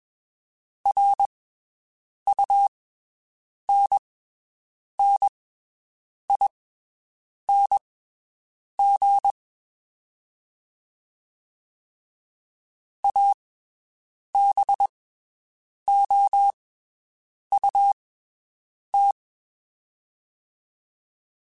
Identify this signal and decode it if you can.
Morse code, heard as a keyed tone, about 21 wpm, spaced as for 7 wpm, 800 Hz. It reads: RUNNING ABOUT
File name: (spaced) / QSOs Spaced-mp3s / WD0FBT 1 @20 sp